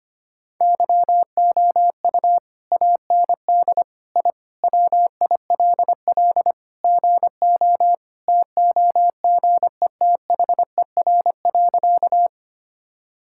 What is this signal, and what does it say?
Morse code, heard as a keyed tone, about 25 wpm, 690 Hz. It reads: YOU AND I WILL GO TOGETHER.
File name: practice / 25wpm / Sentences with 2-Word Phrases - 25wpm